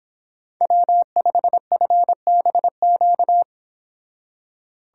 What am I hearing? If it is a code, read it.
Morse code, heard as a keyed tone, about 31 wpm, 700 Hz. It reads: W5FBQ